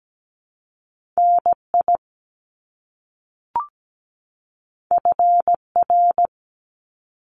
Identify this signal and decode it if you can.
Morse code, heard as a keyed tone, about 17 wpm, 700 Hz. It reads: NI FR
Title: Top Two Letter Combinations in Words 17wpm